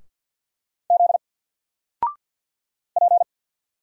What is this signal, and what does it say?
Morse code, heard as a keyed tone, about 50 wpm, 700 Hz. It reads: Z P
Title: Single Letter-Number 50wpm